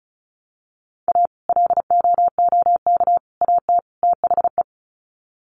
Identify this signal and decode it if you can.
Morse code, heard as a keyed tone, about 34 wpm, 700 Hz. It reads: A LOOK AT THE
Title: Sets of 4 Words 35wpm